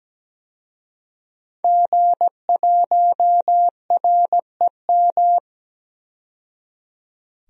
Morse code, heard as a keyed tone, about 17 wpm, 700 Hz. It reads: G1REM
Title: DXCC-100 Call Signs - 5 Characters - 17wpm